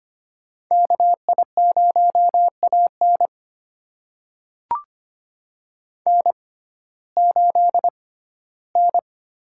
Morse code, heard as a keyed tone, about 25 wpm, spaced as for 15 wpm, 700 Hz. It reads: KI0AN N8N
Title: USA Call Signs - 5 Characters - 6x Character Spacing - 25wpm